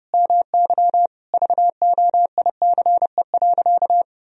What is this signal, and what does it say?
Morse code, heard as a keyed tone, about 30 wpm, 700 Hz. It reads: MY VOICE.
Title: Sentences from Top 1000 Words - Review Part 4 - 30wpm